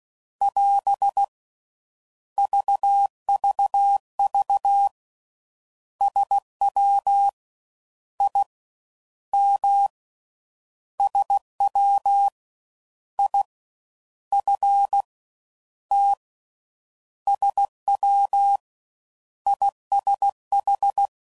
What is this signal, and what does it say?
Morse code, heard as a keyed tone, about 16 wpm, spaced as for 10 wpm, 800 Hz. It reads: <AS> VVV SWIM SWIFT SWISH